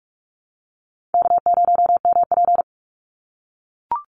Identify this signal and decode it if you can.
Morse code, heard as a keyed tone, about 44 wpm, 690 Hz. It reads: K0MP
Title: NA SOTA Chaser Callsigns 45wpm